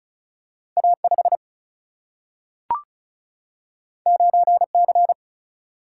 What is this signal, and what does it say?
Morse code, heard as a keyed tone, about 36 wpm, 700 Hz. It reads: A5 9C
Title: Single-Letter Single-Number 35wpm